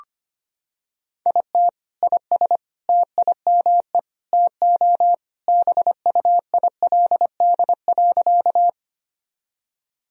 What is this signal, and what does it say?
Morse code, heard as a keyed tone, about 25 wpm, 700 Hz. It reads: IT IS TIME TO BUILD.
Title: Sentences from Top 1000 Words - Review Part 1 - 25wpm